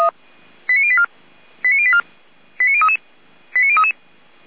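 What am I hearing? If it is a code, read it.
Five-tone selective call (ZVEI), heard as quick runs of five tones, tones of about 70 ms. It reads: 89084 89084 89033 89033
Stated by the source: ZVEI five-tone selective call received by SDR.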